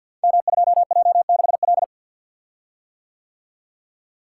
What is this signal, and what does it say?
Morse code, heard as a keyed tone, about 52 wpm, 700 Hz. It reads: M2J6L
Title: Code Groups - 5 Letter:Numbers - 50wpm